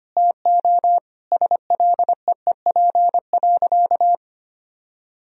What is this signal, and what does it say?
Morse code, heard as a keyed tone, about 25 wpm, 700 Hz. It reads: TO SLEEP.
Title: Sentences from Top 1000 Words - Review Part 2 - 25wpm